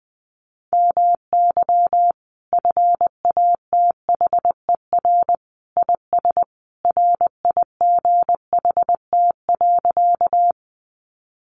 Morse code, heard as a keyed tone, about 19 wpm, 700 Hz. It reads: MY FATHER IS RIGHT.
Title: Sentences from Top 300 Words 20wpm